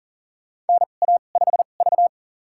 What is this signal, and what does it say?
Morse code, heard as a keyed tone, about 40 wpm, spaced as for 29 wpm, 700 Hz. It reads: NA5V
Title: USA Call Signs - 2x Character Spacing - 40wpm